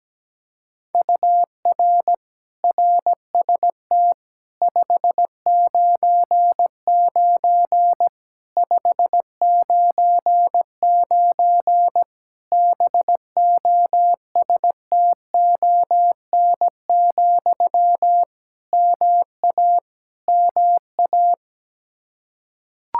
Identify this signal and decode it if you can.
Morse code, heard as a keyed tone, about 17 wpm, 700 Hz. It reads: UR RST 599 599 BOSTON, MA MA
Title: Practice QSO - Exchange 1 - Beginning - 17wpm